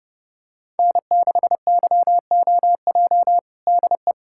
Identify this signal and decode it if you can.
Morse code, heard as a keyed tone, about 32 wpm, 700 Hz. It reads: N6YOJ DE
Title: USA Callsign DE Callsign - 5 Characters - 30wpm